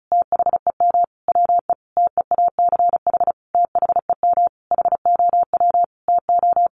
Morse code, heard as a keyed tone, about 34 wpm, 700 Hz. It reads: THEM WE TEACH THEM HOW TO
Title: Sentences from Top 1300 Words 35wpm